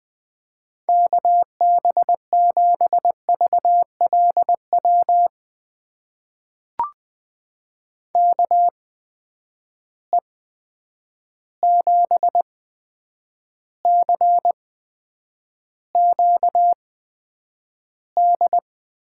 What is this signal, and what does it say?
Morse code, heard as a keyed tone, about 21 wpm, spaced as for 9 wpm, 700 Hz. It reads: KB7VLW KE7CQD
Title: USA Call Signs - 6 Characters - 8x Character Spacing - 20wpm